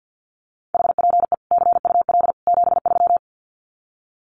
Morse code, heard as a keyed tone, about 50 wpm, 700 Hz. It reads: HPE CUL 73
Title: Common QSO Phrases 50wpm